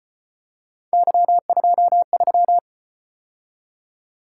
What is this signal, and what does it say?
Morse code, heard as a keyed tone, about 33 wpm, 700 Hz. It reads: Y23